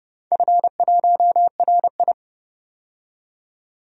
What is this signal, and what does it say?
Morse code, heard as a keyed tone, about 30 wpm, 700 Hz. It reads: F1RI
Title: DXCC-100 Call Signs - 4 Characters - 30wpm